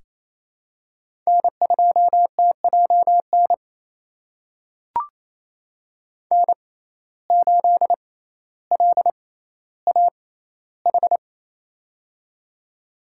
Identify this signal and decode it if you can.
Morse code, heard as a keyed tone, about 27 wpm, spaced as for 14 wpm, 700 Hz. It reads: N2TJN N8LAH
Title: USA Call Signs - 6x Character Spacing - 28wpm